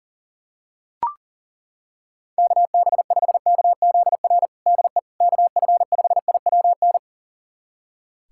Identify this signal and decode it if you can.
Morse code, heard as a keyed tone, about 40 wpm, 700 Hz. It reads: KB5KZR DE KF5IWN
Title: USA Callsign DE Callsign - 6 Characters - 40wpm